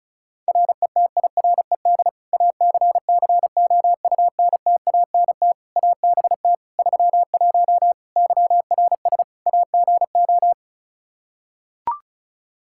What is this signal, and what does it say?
Morse code, heard as a keyed tone, about 36 wpm, 700 Hz. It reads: RETIRED ACCOUNTANT ABT 31 YRS AGO
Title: K1USN SST QSO Practice 35wpm